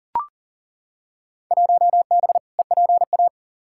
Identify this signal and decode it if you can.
Morse code, heard as a keyed tone, about 40 wpm, 700 Hz. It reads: 1B EPA
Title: Field Day Exchanges Encore - 40wpm